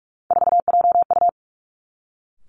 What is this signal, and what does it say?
Morse code, heard as a keyed tone, about 46 wpm, 700 Hz. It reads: 4JU
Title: Three Challenging Letter-Number Combinations 45wpm